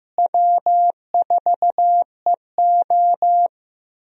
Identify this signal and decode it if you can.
Morse code, heard as a keyed tone, about 15 wpm, 700 Hz. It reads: W4EO
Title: USA Call Signs - 4 Characters - Repeated 3x - 15wpm